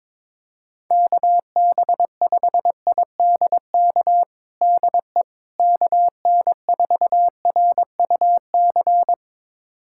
Morse code, heard as a keyed tone, about 22 wpm, 700 Hz. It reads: KB5IDK DE KN4RUC
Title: USA Callsign DE Callsign - 6 Characters - 22wpm